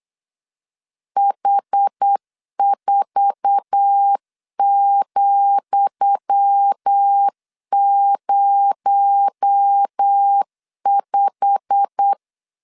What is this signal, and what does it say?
Morse code, heard as a keyed tone, about 8 wpm, 800 Hz. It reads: H4,05
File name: LEZIONI / 10 SIGNS / 10D COMA 8wpm 3749